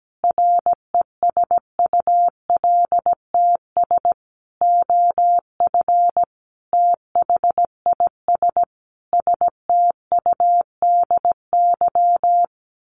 Morse code, heard as a keyed tone, about 17 wpm, 700 Hz. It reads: RESULTS OF THIS STUDY
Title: Sets of 5 Words 17wpm